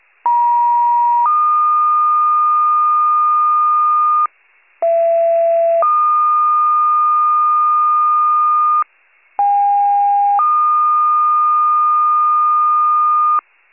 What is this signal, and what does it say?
Two-tone page, heard as a one-second tone, then a three-second tone, about 950 then 1220 Hz, about 670 then 1220 Hz, about 810 then 1220 Hz. It is heard from 0.2 to 13.4 s.
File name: wcalert